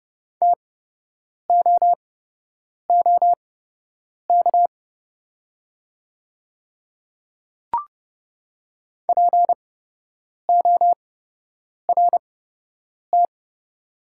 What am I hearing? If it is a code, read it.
Morse code, heard as a keyed tone, about 30 wpm, spaced as for 8 wpm, 700 Hz. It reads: TOOK PORT
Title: 8x Character Spacing - Top 300 Words - 30wpm